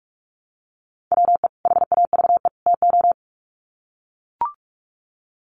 Morse code, heard as a keyed tone, about 44 wpm, 700 Hz. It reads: WE HAVE TO
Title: Sets of 3 Words 45wpm